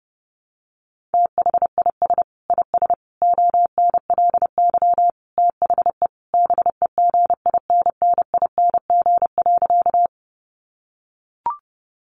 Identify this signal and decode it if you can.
Morse code, heard as a keyed tone, about 30 wpm, 700 Hz. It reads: THIS IS ONLY THE BEGINNING.
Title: Sentences from Top 800 Words 30wpm